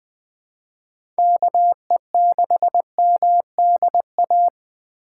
Morse code, heard as a keyed tone, about 20 wpm, 700 Hz. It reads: KE6MDA
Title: USA Call Signs - 2x Character Spacing - 20wpm